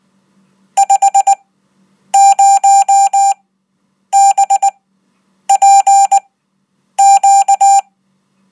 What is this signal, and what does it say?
Morse code, heard as a keyed tone, about 19 wpm, spaced as for 8 wpm, 790 Hz. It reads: 50BPQ
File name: Code